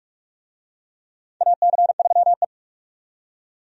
Morse code, heard as a keyed tone, about 44 wpm, 700 Hz. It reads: AC3E
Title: USA Call Signs - 4 Characters - 45wpm